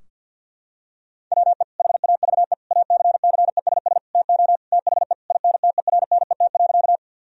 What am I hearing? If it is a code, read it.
Morse code, heard as a keyed tone, about 50 wpm, 700 Hz. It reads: WE HAVE ACCESS TO THE INTERNET.
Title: Sentences from Top 1000 Words - Review Part 4 - 50wpm